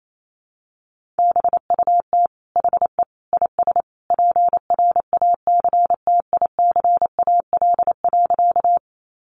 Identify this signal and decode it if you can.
Morse code, heard as a keyed tone, about 28 wpm, 700 Hz. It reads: BUT HE IS PRACTICAL.
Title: Sentences from Top 1400 Words 28wpm